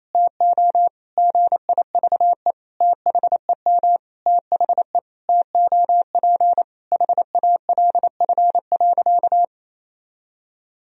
Morse code, heard as a keyed tone, about 28 wpm, 700 Hz. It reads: TO GIVE THEM THE TOP HALF.
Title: Sentences from Top 1000 Words - Review Part 2 - 28wpm